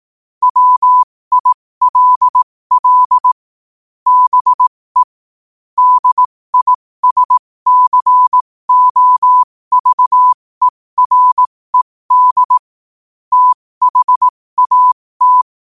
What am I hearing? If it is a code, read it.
Morse code, heard as a keyed tone, about 18 wpm, spaced as for 15 wpm, 1000 Hz. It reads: WILL BE DISCOVERED THAT